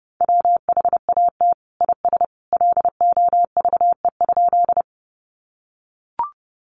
Morse code, heard as a keyed tone, about 30 wpm, 700 Hz. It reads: WHAT IS LOVE?
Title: Sentences from Top 400 Words 30wpm